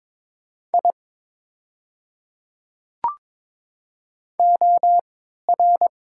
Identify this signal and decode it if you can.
Morse code, heard as a keyed tone, about 22 wpm, spaced as for 12 wpm, 700 Hz. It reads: I OR
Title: 3x Character Spacing - Top 100 Words - 22wpm